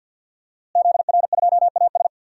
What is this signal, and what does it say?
Morse code, heard as a keyed tone, about 48 wpm, 700 Hz. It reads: ZR2AS
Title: DXCC-100 Call Signs - 5 Characters - Repeated 3x - 50wpm